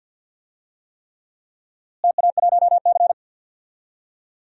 Morse code, heard as a keyed tone, about 49 wpm, 700 Hz. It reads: TA1C
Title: DXCC-100 Call Signs - 4 Characters - Repeated 3x - 50wpm